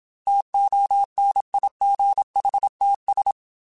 Morse code, heard as a keyed tone, about 27 wpm, 800 Hz. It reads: TONIGHTS
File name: Weather Sets / Weather Sets-mp3s / Weather 5 5 @25